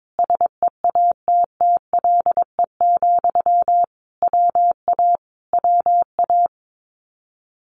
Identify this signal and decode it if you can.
Morse code, heard as a keyed tone, about 22 wpm, 700 Hz. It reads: SEATTLE, WA WA
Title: Common QSO Phrases Encore 22wpm